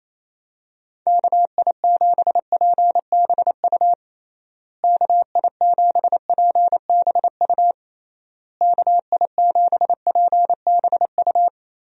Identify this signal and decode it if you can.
Morse code, heard as a keyed tone, about 28 wpm, 700 Hz. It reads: KI7PBU KI7PBU KI7PBU
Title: USA Call Signs - 6 Characters - Repeated 3x - 28wpm